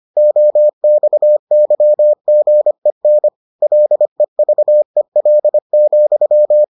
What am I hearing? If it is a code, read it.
Morse code, heard as a keyed tone, about 25 wpm, 600 Hz. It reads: OXYGEN LEVEL,